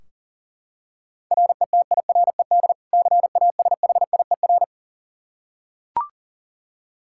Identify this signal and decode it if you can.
Morse code, heard as a keyed tone, about 40 wpm, 700 Hz. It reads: RETIRED CASHIER
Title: Practice QSO - Exchange 3 - More Info - 40wpm